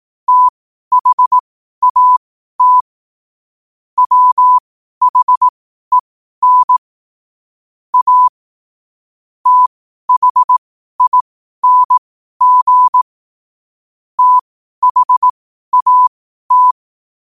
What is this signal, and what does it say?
Morse code, heard as a keyed tone, about 18 wpm, spaced as for 12 wpm, 1000 Hz. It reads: THAT WHEN A THING THAT
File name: qotdfiles / 12wpm